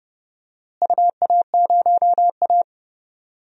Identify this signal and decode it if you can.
Morse code, heard as a keyed tone, about 30 wpm, 700 Hz. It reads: UA0A
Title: DXCC-100 Call Signs - 4 Characters - Repeated 3x - 30wpm